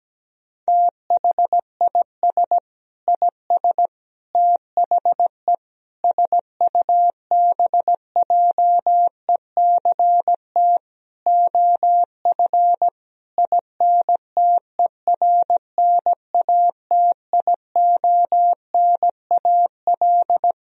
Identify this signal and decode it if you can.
Morse code, heard as a keyed tone, about 17 wpm, 700 Hz. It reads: THIS IS THE SUBJECT OF INTERNATIONAL
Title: Sentences from Top 1000 Words - Review Part 4 - 17wpm